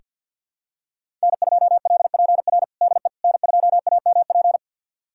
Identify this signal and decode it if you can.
Morse code, heard as a keyed tone, about 49 wpm, 700 Hz. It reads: N2LPR DE N2AMP
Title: USA Callsign DE Callsign - 5 Characters - 50wpm